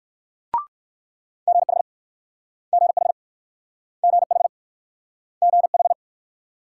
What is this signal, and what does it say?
Morse code, heard as a keyed tone, about 63 wpm, 700 Hz. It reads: GH GH GH GH